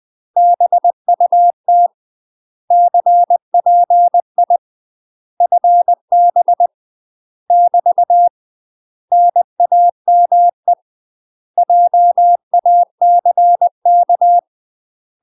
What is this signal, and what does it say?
Morse code, heard as a keyed tone, about 20 wpm, 700 Hz. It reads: BUT CPI FB <BT> NAME JACK
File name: UK MASTER QSOs TRAINING / 20WPM / UK MASTER QSOs 0038